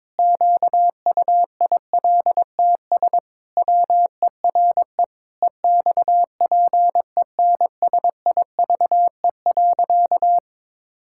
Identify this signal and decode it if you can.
Morse code, heard as a keyed tone, about 22 wpm, 700 Hz. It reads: QUILTS WERE EXPENSIVE.
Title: Pangram Challenge v2 - 22wpm